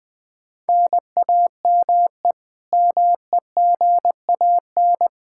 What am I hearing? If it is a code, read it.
Morse code, heard as a keyed tone, about 21 wpm, 700 Hz. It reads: NAME MEGAN
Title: Practice QSO - Exchange 1 - Beginning - 20wpm